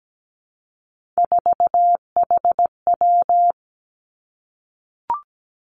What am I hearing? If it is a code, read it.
Morse code, heard as a keyed tone, about 17 wpm, 700 Hz. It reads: 4HW
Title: Three Challenging Letter-Number Combinations 17wpm